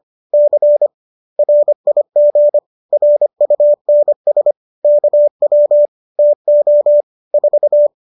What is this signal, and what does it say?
Morse code, heard as a keyed tone, about 25 wpm, 590 Hz. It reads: C RIG RUNS KW TO 4